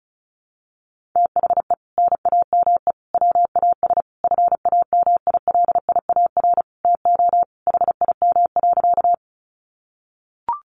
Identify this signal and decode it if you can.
Morse code, heard as a keyed tone, about 34 wpm, 700 Hz. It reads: THE NAME WAS FAMILIAR TO HIM.
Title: Sentences from Top 1000 Words 35wpm